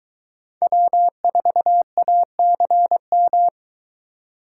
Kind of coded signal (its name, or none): Morse code